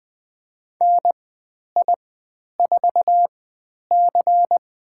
Morse code, heard as a keyed tone, about 20 wpm, spaced as for 10 wpm, 700 Hz. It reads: NI4C
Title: 102 Lesson 17g - Letter u - Call Signs 20wpm